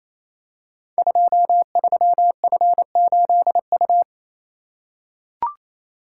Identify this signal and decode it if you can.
Morse code, heard as a keyed tone, about 29 wpm, 700 Hz. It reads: 23F8U
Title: Code Groups - 5 Letter:Numbers - 28wpm